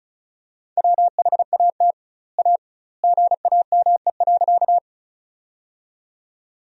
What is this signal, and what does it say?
Morse code, heard as a keyed tone, about 36 wpm, 700 Hz. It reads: WHAT A GAME.
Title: 2x Word-Spacing - Sentences from Top 500 Words - 35wpm